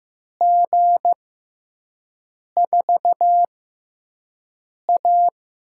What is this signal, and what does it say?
Morse code, heard as a keyed tone, about 15 wpm, spaced as for 5 wpm, 700 Hz. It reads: G4A
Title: USA Call Signs - 6x Character Spacing - 15wpm